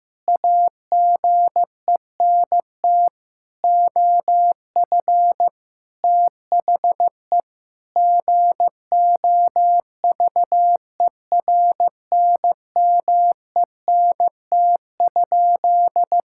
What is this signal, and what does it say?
Morse code, heard as a keyed tone, about 15 wpm, 700 Hz. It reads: AGENT OF THE GOVERNMENT?
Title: Sentences from Top 1600 Words 15wpm